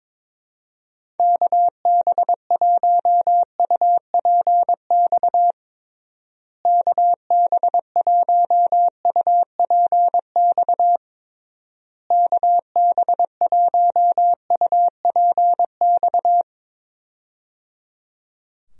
Morse code, heard as a keyed tone, about 22 wpm, 700 Hz. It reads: KB1UPX KB1UPX KB1UPX
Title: USA Call Signs - 6 Characters - Repeated 3x - 22wpm